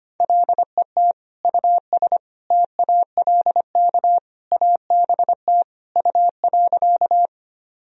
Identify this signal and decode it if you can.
Morse code, heard as a keyed tone, about 25 wpm, 700 Hz. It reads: LET US TALK ABT U.
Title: Sentences from Top 400 Words and Abbreviations 25wpm